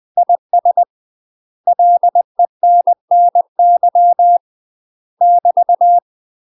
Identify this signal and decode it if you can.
Morse code, heard as a keyed tone, about 20 wpm, 700 Hz. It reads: IS LENNY <BT>
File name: UK MASTER QSOs TRAINING / 20WPM / UK MASTER QSOs 0007